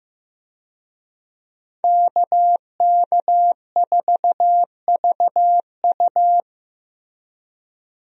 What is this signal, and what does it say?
Morse code, heard as a keyed tone, about 16 wpm, 700 Hz. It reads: KK4VU
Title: USA Call Signs - 5 Characters - Repeated 2x - 15wpm